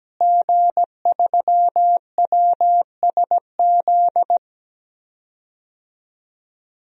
Morse code, heard as a keyed tone, about 17 wpm, 700 Hz. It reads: G3WSZ
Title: DXCC-100 Call Signs - 5 Characters - Repeated 3x - 17wpm